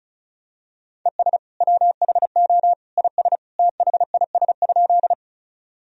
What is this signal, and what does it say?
Morse code, heard as a keyed tone, about 34 wpm, 700 Hz. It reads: ES WHO IS THIS?
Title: Sentences from Top 100 Words and Abbreviations 35wpm